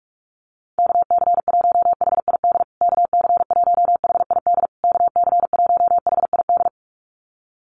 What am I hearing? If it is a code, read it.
Morse code, heard as a keyed tone, about 44 wpm, 690 Hz. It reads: KC1HID KC1HID KC1HID
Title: USA Call Signs Repeated 3x 45wpm